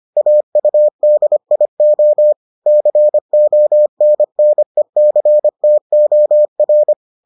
Morse code, heard as a keyed tone, about 25 wpm, 600 Hz. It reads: AUDIO CONNECTOR